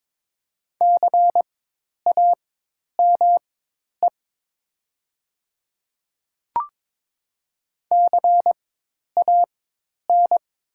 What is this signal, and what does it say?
Morse code, heard as a keyed tone, about 22 wpm, spaced as for 10 wpm, 690 Hz. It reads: CAME CAN
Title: 4x Character Spacing - Top 500 Words - 22wpm